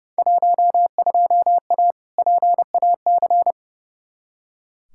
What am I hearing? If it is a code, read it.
Morse code, heard as a keyed tone, about 30 wpm, 700 Hz. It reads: 12A PAC
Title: Field Day Exchanges Encore - 30wpm